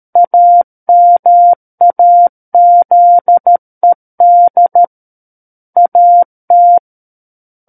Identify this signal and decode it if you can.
Morse code, heard as a keyed tone, about 13 wpm, 700 Hz. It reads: AMAZED AT
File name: Book One / 04-Chapter 4 - 13 WPM